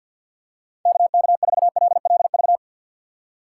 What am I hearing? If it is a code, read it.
Morse code, heard as a keyed tone, about 52 wpm, 690 Hz. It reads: KK4LLV
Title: USA Call Signs - 3x Character Spacing - 50wpm